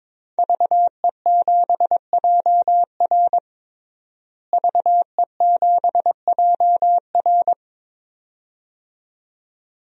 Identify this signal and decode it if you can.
Morse code, heard as a keyed tone, about 22 wpm, 700 Hz. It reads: VE7JR VE7JR
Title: DXCC-100 Call Signs - 5 Characters - Repeated 2x - 22wpm